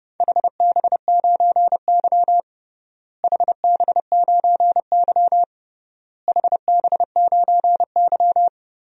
Morse code, heard as a keyed tone, about 30 wpm, 700 Hz. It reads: HB9Y HB9Y HB9Y
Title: DXCC-100 Call Signs - 4 Characters - Repeated 3x - 30wpm